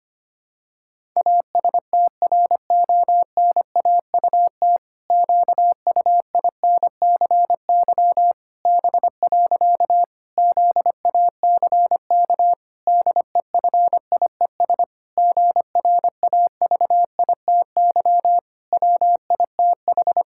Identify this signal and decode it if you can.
Morse code, heard as a keyed tone, about 25 wpm, 700 Hz. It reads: ASTRONAUT QUINCY B. ZACK DEFIES GRAVITY WITH
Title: Pangram Challenge v2 - 25wpm